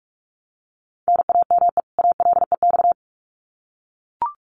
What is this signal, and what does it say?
Morse code, heard as a keyed tone, about 44 wpm, 700 Hz. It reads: NAME ALEX
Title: Common QSO Phrases 45wpm